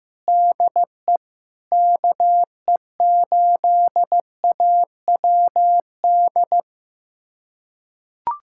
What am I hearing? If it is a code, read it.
Morse code, heard as a keyed tone, about 15 wpm, 700 Hz. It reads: DE KE8AWD
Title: USA Callsign DE Callsign - 6 Characters - 15wpm